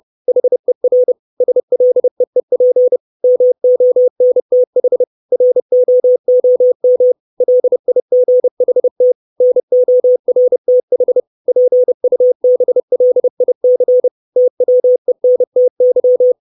Morse code, heard as a keyed tone, about 30 wpm, 500 Hz. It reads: HER SLEEP MONTH ROOM LIGHT NORTH PUBLIC TWENTY
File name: 632 Words 30WPM